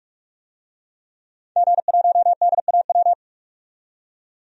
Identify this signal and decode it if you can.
Morse code, heard as a keyed tone, about 42 wpm, 700 Hz. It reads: G1DAW
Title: DXCC-100 Call Signs - 5 Characters - 45wpm